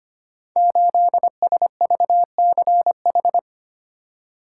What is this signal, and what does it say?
Morse code, heard as a keyed tone, about 25 wpm, 700 Hz. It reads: 8SVCH
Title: Code Groups - 5 Letter:Numbers - 25wpm